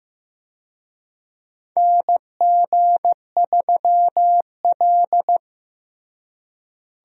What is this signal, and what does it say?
Morse code, heard as a keyed tone, about 15 wpm, 700 Hz. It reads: NG3L
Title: USA Call Signs - 4 Characters - Repeated 2x - 15wpm